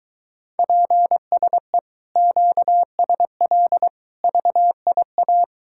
Morse code, heard as a keyed tone, about 23 wpm, 700 Hz. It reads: PSE QSL VIA